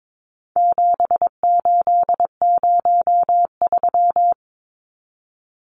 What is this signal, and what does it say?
Morse code, heard as a keyed tone, about 22 wpm, 700 Hz. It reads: 7803
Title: Numbers - 4-Digit 22wpm